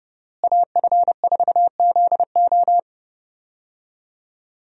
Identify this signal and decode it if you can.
Morse code, heard as a keyed tone, about 30 wpm, 700 Hz. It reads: AF4ZO
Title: USA Call Signs - 5 Characters - Repeated 3x - 30wpm